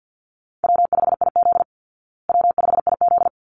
Morse code, heard as a keyed tone, about 49 wpm, 700 Hz. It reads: W5IZ W5IZ
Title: Call Signs Repeated 3x - 50wpm